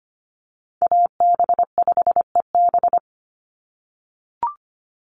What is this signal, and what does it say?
Morse code, heard as a keyed tone, about 25 wpm, 700 Hz. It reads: AB5EB
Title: Call Signs - 25wpm